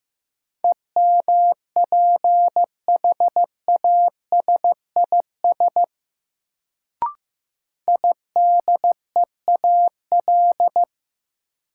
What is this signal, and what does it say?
Morse code, heard as a keyed tone, about 15 wpm, 700 Hz. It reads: EMPHASIS IDEAL
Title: Sentences from Top 1400 Words - New Word Drill - 15wpm